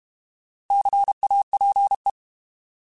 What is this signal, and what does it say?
Morse code, heard as a keyed tone, about 32 wpm, 800 Hz. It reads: CAPE
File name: Alphabetical / Alphabetical mp3s / Cs @30